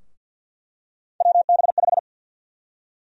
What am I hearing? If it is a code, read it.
Morse code, heard as a keyed tone, about 48 wpm, 700 Hz. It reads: WB5
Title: USA Call Sign Prefixes - 50wpm